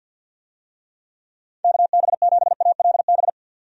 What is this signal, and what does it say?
Morse code, heard as a keyed tone, about 50 wpm, 700 Hz. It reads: KB7ALB
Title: USA Call Signs - 6 Characters - 50wpm